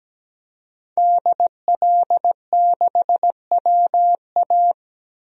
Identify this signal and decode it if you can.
Morse code, heard as a keyed tone, about 17 wpm, 700 Hz. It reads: DL6WA
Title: DXCC-100 Call Signs - 5 Characters - 17wpm